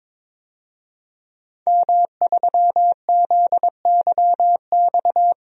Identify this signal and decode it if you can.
Morse code, heard as a keyed tone, about 22 wpm, 700 Hz. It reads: M3ZYX
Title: DXCC-100 Call Signs - 5 Characters - 22wpm